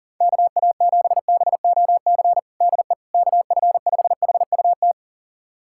Morse code, heard as a keyed tone, about 37 wpm, 700 Hz. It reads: KA7BOC DE KF5HUT
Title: USA Callsign DE Callsign - 6 Characters - 40wpm